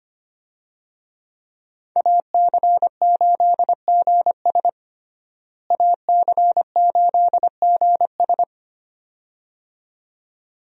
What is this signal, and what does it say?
Morse code, heard as a keyed tone, about 25 wpm, 700 Hz. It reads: AC8GS AC8GS
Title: USA Call Signs - 5 Characters - Repeated 2x - 25wpm